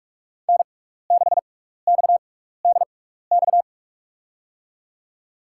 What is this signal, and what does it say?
Morse code, heard as a keyed tone, about 46 wpm, spaced as for 16 wpm, 700 Hz. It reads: N6XDX
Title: USA Call Signs - 5 Characters - 6x Character Spacing - 45wpm